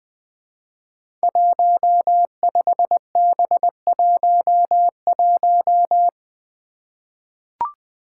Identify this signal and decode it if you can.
Morse code, heard as a keyed tone, about 20 wpm, 700 Hz. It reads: 15B11
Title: Five Challenging Letter-Number Combinations 20wpm